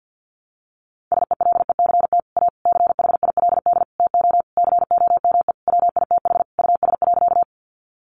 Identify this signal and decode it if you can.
Morse code, heard as a keyed tone, about 50 wpm, 700 Hz. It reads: SELECT A CHILD TO COME WITH US.
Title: Sentences from Top 1300 Words 50wpm